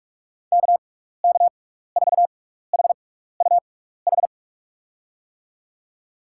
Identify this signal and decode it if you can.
Morse code, heard as a keyed tone, about 43 wpm, spaced as for 15 wpm, 700 Hz. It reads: KK4HUH
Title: USA Call Signs - 6 Characters - 6x Character Spacing - 45wpm